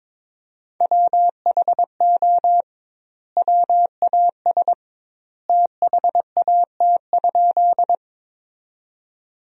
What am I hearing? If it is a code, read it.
Morse code, heard as a keyed tone, about 22 wpm, 700 Hz. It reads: WHO WAS THAT?
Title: 2x Word-Spacing - Sentences from Top 100 Words - 22wpm